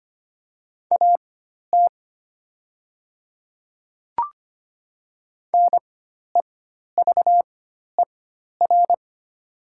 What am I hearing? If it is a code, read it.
Morse code, heard as a keyed tone, about 25 wpm, spaced as for 12 wpm, 700 Hz. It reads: AT NEVER
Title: 4x Character Spacing - Top 300 Words - 25wpm